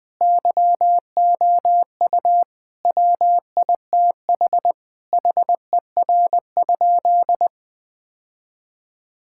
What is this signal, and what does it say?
Morse code, heard as a keyed tone, about 20 wpm, 700 Hz. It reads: YOU WITH HER?
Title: Sentences from Top 1000 Words - Review Part 5 - 20wpm